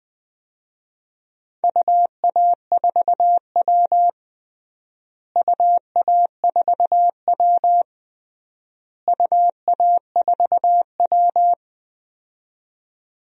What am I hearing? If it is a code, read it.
Morse code, heard as a keyed tone, about 20 wpm, 690 Hz. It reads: UA4W UA4W UA4W
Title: DXCC-100 Call Signs - 4 Characters - Repeated 3x - 20wpm